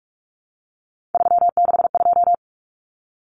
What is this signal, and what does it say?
Morse code, heard as a keyed tone, about 44 wpm, 700 Hz. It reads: 362
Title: Common Cut Numbers - 3-Digit 45wpm